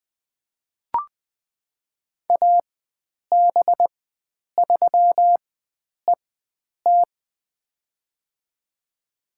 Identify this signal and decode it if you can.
Morse code, heard as a keyed tone, about 20 wpm, spaced as for 9 wpm, 700 Hz. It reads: AB3ET
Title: USA Call Signs - 4x Character Spacing - 20wpm